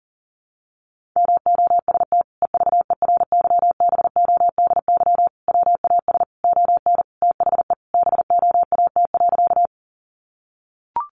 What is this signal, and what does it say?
Morse code, heard as a keyed tone, about 40 wpm, 700 Hz. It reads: MOST EVERYBODY WAS ON THE BOAT.
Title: Sentences from Top 1400 Words 40wpm